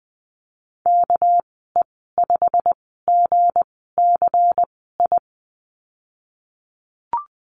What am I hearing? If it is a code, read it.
Morse code, heard as a keyed tone, about 20 wpm, spaced as for 15 wpm, 700 Hz. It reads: KE5GCI
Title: 207 Lesson 33g - Letter j - Call Signs 20wpm